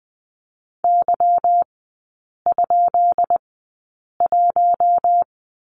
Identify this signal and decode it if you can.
Morse code, heard as a keyed tone, about 20 wpm, spaced as for 8 wpm, 690 Hz. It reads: Y?1
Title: 205 Lesson 33e - Letter j - Three Characters 20wpm